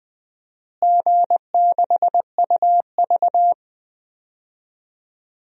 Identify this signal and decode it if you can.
Morse code, heard as a keyed tone, about 20 wpm, 700 Hz. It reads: G6UV
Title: DXCC-100 Call Signs - 4 Characters - Repeated 3x - 20wpm